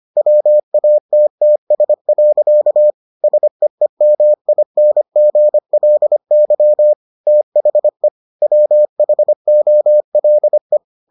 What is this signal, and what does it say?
Morse code, heard as a keyed tone, about 25 wpm, 600 Hz. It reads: WATTS. SEEMINGLY THE WHOLE